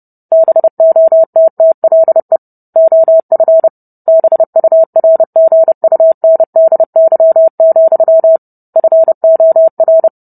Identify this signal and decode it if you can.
Morse code, heard as a keyed tone, about 30 wpm, 650 Hz. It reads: BOTTLE OF BURGUNDY, FOR